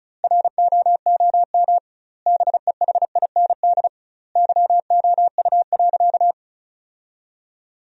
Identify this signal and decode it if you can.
Morse code, heard as a keyed tone, about 36 wpm, 700 Hz. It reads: ROOM BEHIND YOU.
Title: 2x Word-Spacing - Sentences from Top 500 Words - 35wpm